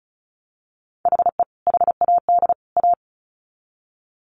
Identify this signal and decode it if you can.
Morse code, heard as a keyed tone, about 34 wpm, 700 Hz. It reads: HE HAD A
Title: Sets of 3 Words 35wpm